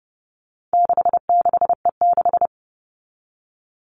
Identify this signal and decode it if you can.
Morse code, heard as a keyed tone, about 29 wpm, 700 Hz. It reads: 66E6
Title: All Cut Numbers - 4-Digit 30wpm